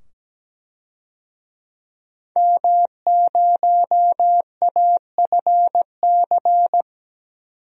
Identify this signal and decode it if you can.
Morse code, heard as a keyed tone, about 17 wpm, 700 Hz. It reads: M0AFC
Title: DXCC-100 Call Signs - 5 Characters - 17wpm